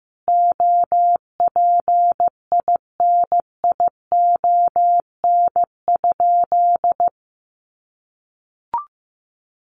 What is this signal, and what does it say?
Morse code, heard as a keyed tone, about 15 wpm, 700 Hz. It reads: OPINION?
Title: Sentences from Top 1200 Words 15wpm